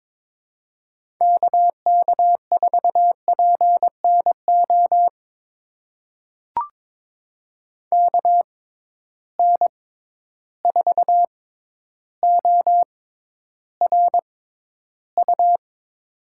Morse code, heard as a keyed tone, about 22 wpm, spaced as for 11 wpm, 700 Hz. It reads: KK4PNO KN4ORU
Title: USA Call Signs - 6 Characters - 6x Character Spacing - 22wpm